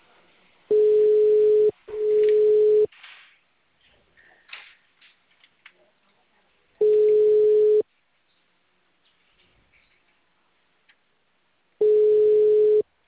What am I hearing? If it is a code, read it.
Ringback tone (Europe, 425 Hz).